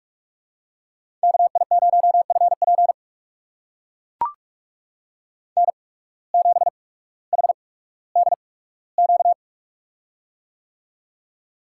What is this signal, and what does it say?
Morse code, heard as a keyed tone, about 44 wpm, spaced as for 19 wpm, 700 Hz. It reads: KI0FP N7HDQ
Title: USA Call Signs - 8x Character Spacing - 45wpm